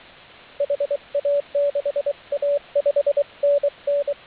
Morse code, heard as a keyed tone, about 23 wpm, 570 Hz, with radio noise. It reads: HA6A 5NN